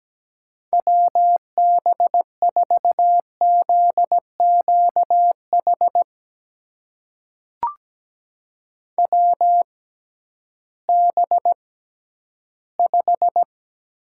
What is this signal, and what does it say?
Morse code, heard as a keyed tone, about 17 wpm, spaced as for 11 wpm, 700 Hz. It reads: WB4ZQH WB5
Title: USA Call Signs - 6 Characters - 6x Character Spacing - 17wpm